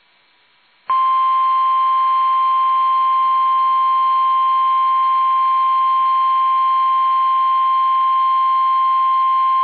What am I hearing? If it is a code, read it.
Emergency Alert System (SAME) tones (the 1050 Hz attention tone). It is heard from 1.2 to 9.0 s.